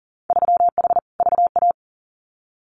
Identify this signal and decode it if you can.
Morse code, heard as a keyed tone, about 38 wpm, 700 Hz. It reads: 3H VA
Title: Winter Field Day Exchanges 40wpm